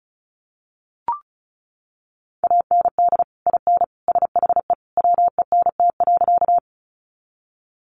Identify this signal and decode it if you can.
Morse code, heard as a keyed tone, about 34 wpm, 700 Hz. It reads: AND IN SHE WENT.
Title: Sentences from Top 200 Words 35wpm